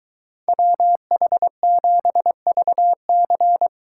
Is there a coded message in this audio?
Yes: Morse code from 0.5 to 3.7 s, heard as a keyed tone, about 23 wpm, 700 Hz. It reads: WH7VC